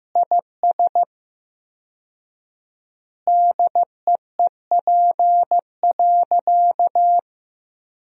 Morse code, heard as a keyed tone, about 15 wpm, 700 Hz. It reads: IS DEEP.
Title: 4x Word-Spacing - Sentences from Top 500 Words - 15wpm